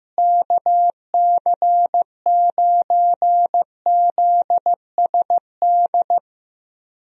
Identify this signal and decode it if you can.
Morse code, heard as a keyed tone, about 15 wpm, 700 Hz. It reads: KC9ZSD
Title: USA Call Signs - 6 Characters - Repeated 3x - 15wpm